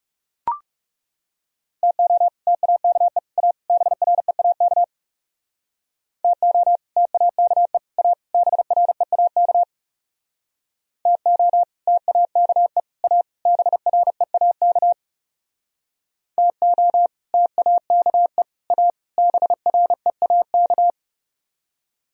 Morse code, heard as a keyed tone, about 40 wpm, 690 Hz. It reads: TO TAKE A BREAK TO TAKE A BREAK TO TAKE A BREAK TO TAKE A BREAK